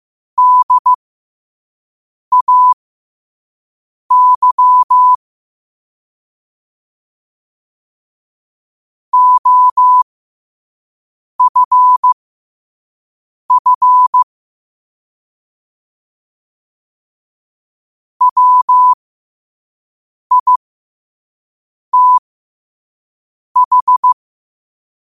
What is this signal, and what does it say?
Morse code, heard as a keyed tone, about 15 wpm, spaced as for 5 wpm, 1000 Hz. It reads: DAY OFF WITH
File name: qotdfiles / 5wpm